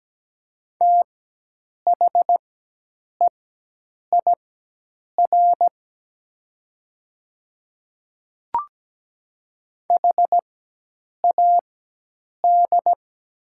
Morse code, heard as a keyed tone, about 17 wpm, spaced as for 8 wpm, 700 Hz. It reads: THEIR HAD
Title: 4x Character Spacing - Top 100 Words - 17wpm